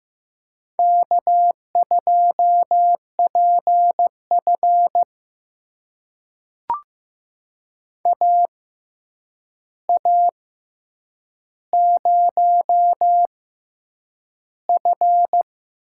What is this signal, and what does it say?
Morse code, heard as a keyed tone, about 15 wpm, spaced as for 8 wpm, 700 Hz. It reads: K2PF AA0F
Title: USA Call Signs - 4 Characters - 6x Character Spacing - 15wpm